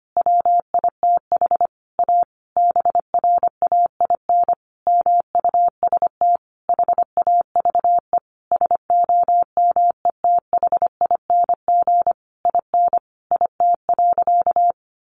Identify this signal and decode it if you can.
Morse code, heard as a keyed tone, about 25 wpm, 700 Hz. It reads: WITH A BRAIN MUST HAVE SOMETHING IN IT.
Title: Sentences from Top 1100 Words 25wpm